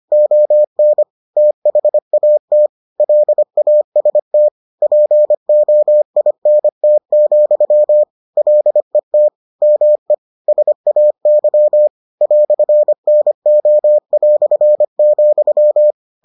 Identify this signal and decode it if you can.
Morse code, heard as a keyed tone, about 25 wpm, 600 Hz. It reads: ON THAT LAST POINT, LET ME SAY "NO",